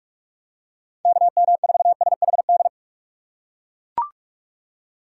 Morse code, heard as a keyed tone, about 42 wpm, 700 Hz. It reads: KM4SHD
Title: USA Call Signs - 2x Character Spacing - 45wpm